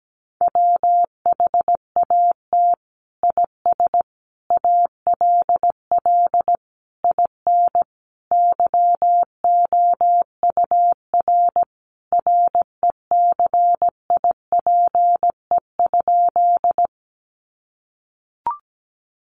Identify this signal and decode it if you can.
Morse code, heard as a keyed tone, about 17 wpm, 700 Hz. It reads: WHAT IS ALL IN YOUR RECIPE?
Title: Sentences from Top 1200 Words 17wpm